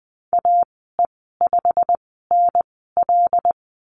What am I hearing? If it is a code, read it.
Morse code, heard as a keyed tone, about 20 wpm, spaced as for 14 wpm, 700 Hz. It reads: AE5NL
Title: 128 Lesson 21g - Number 6 - Call Signs 20wpm